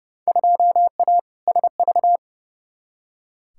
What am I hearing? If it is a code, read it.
Morse code, heard as a keyed tone, about 30 wpm, 700 Hz. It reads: 2A SV
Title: Field Day Exchanges - 30wpm